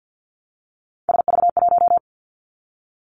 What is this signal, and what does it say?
Morse code, heard as a keyed tone, about 50 wpm, 700 Hz. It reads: SV1
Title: Callsign Prefixes for Top 50 DXCC - 50wpm